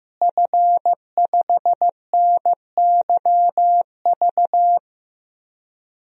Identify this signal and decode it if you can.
Morse code, heard as a keyed tone, about 15 wpm, 700 Hz. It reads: F5NYV
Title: DXCC-100 Call Signs - 5 Characters - Repeated 3x - 15wpm